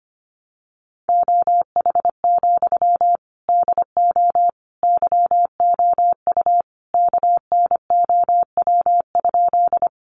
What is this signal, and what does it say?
Morse code, heard as a keyed tone, about 25 wpm, 700 Hz. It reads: OH, DO YOU KNOW?
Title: Sentences from Top 500 Words 25wpm